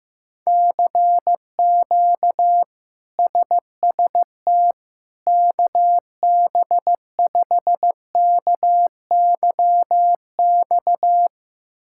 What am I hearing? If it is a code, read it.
Morse code, heard as a keyed tone, about 15 wpm, 700 Hz. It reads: CQ SST KB5KYX
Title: K1USN SST QSO Practice - 15wpm